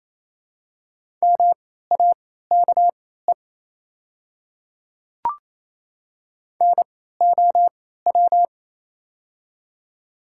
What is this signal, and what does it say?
Morse code, heard as a keyed tone, about 28 wpm, spaced as for 16 wpm, 700 Hz. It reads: MAKE NOW
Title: 3x Character Spacing - Top 300 Words - 28wpm